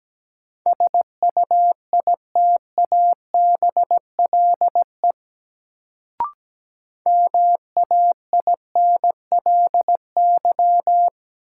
Morse code, heard as a keyed tone, about 17 wpm, 700 Hz. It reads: SUITABLE MAINLY
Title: Sentences from Top 1400 Words - New Word Drill Alt - 17wpm